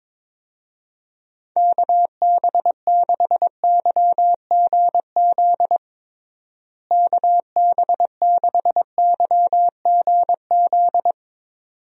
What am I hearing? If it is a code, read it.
Morse code, heard as a keyed tone, about 23 wpm, 700 Hz. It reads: KB6YGZ KB6YGZ
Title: USA Call Signs - 6 Characters - Repeated 2x - 22wpm